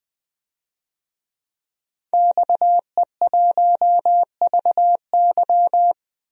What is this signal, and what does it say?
Morse code, heard as a keyed tone, about 20 wpm, 700 Hz. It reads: XE1VY
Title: DXCC-100 Call Signs - 5 Characters - Repeated 2x - 20wpm